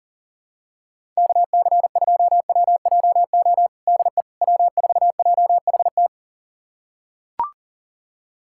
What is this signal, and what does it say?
Morse code, heard as a keyed tone, about 37 wpm, 700 Hz. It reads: KC2WJO DE W4JHT
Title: USA Callsign DE Callsign 40wpm